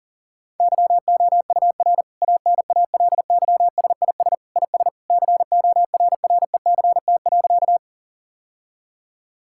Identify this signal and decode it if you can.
Morse code, heard as a keyed tone, about 40 wpm, 700 Hz. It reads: YOUR ANALYSIS IS CORRECT.
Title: Sentences from Top 1000 Words - Review Part 5 - 40wpm